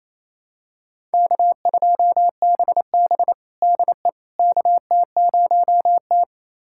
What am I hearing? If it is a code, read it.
Morse code, heard as a keyed tone, about 28 wpm, 700 Hz. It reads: K2BB DE KT0T
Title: USA Callsign DE Callsign - 4 Characters - 28wpm